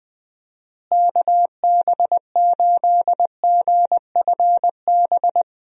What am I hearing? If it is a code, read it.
Morse code, heard as a keyed tone, about 21 wpm, 700 Hz. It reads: KB8GFB
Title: USA Call Signs - 6 Characters - Repeated 2x - 20wpm